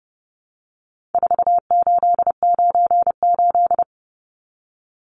Morse code, heard as a keyed tone, about 30 wpm, 700 Hz. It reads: 4898
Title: Numbers - 4-Digit 30wpm